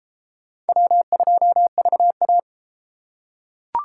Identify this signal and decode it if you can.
Morse code, heard as a keyed tone, about 30 wpm, 700 Hz. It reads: W2VA